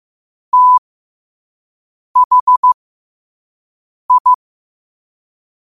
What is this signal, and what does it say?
Morse code, heard as a keyed tone, about 15 wpm, spaced as for 5 wpm, 1000 Hz. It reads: THI
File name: qotdfiles / 5wpm